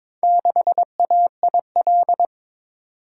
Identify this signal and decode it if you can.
Morse code, heard as a keyed tone, about 22 wpm, 700 Hz. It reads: 6AIL
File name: practice / 22wpm / Four Letter-Number Combinations - 22wpm